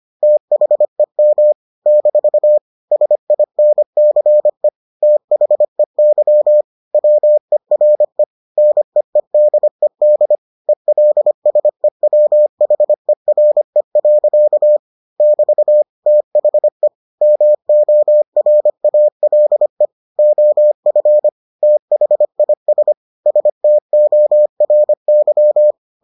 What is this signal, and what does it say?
Morse code, heard as a keyed tone, about 25 wpm, 600 Hz. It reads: THEM - SINCE THEY WERE NEEDED ELSEWHERE. <BT> THE MORALE OF THIS STORY